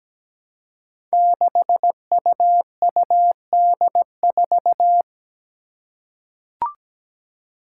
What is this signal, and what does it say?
Morse code, heard as a keyed tone, about 17 wpm, 700 Hz. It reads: 6UUD4
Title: Five Challenging Letter-Number Combinations 17wpm